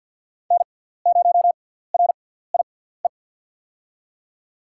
Morse code, heard as a keyed tone, about 52 wpm, spaced as for 17 wpm, 700 Hz. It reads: N0RIE
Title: USA Call Signs - 5 Characters - 6x Character Spacing - 50wpm